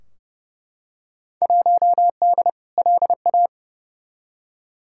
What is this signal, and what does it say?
Morse code, heard as a keyed tone, about 30 wpm, 700 Hz. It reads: 1D LA
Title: Field Day Exchanges - 30wpm